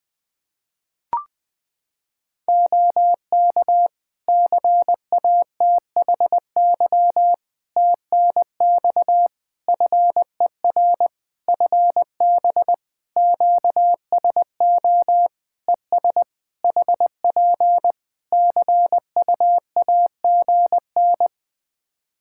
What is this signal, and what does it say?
Morse code, heard as a keyed tone, about 21 wpm, 700 Hz. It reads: OK CATHY TNX FER FB QSO ES HP CUAGN
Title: K1USN SST QSO Practice 20wpm